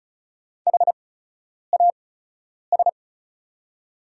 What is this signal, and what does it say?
Morse code, heard as a keyed tone, about 36 wpm, spaced as for 10 wpm, 690 Hz. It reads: HAS
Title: 8x Character Spacing - Top 200 Words - 35wpm